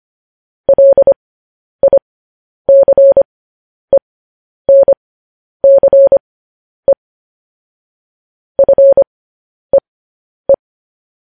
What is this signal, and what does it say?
Morse code, heard as a keyed tone, about 25 wpm, spaced as for 10 wpm, 550 Hz. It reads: LICENCE FEE
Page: News Headlines in Morse Code at 10 WPM